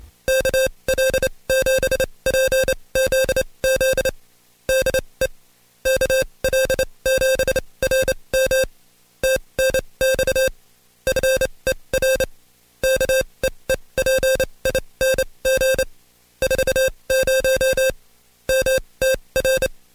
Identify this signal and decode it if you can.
Morse code, heard as a keyed tone, about 27 wpm, spaced as for 22 wpm, 530 Hz, with noise. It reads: KL7PZZ DE KL7RM TNX FER KEEPING 40 MTR
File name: Code Quick / AudioDownld2 / Seg137-Stepper10 21WPM